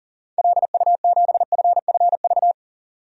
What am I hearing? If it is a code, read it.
Morse code, heard as a keyed tone, about 40 wpm, 700 Hz. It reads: LU7FFV
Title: DXCC-100 Call Signs - 6 Characters - 40wpm